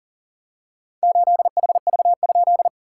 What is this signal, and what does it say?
Morse code, heard as a keyed tone, about 42 wpm, 690 Hz. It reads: 8HV?
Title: SOTA QSO Practice 40wpm